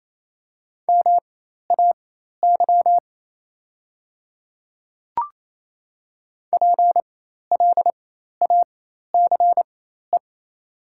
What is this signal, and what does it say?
Morse code, heard as a keyed tone, about 28 wpm, spaced as for 13 wpm, 700 Hz. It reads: MAY PLACE
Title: 4x Character Spacing - Top 300 Words - 28wpm